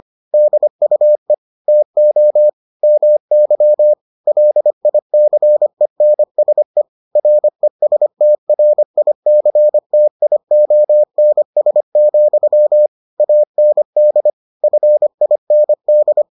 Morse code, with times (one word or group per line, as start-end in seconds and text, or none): 0.3-1.3 DUE
1.7-2.5 TO
2.8-3.9 MY
4.3-6.8 LICENSE
7.2-12.9 RESTRICTIONS,
13.2-14.3 AND
14.6-16.2 FIND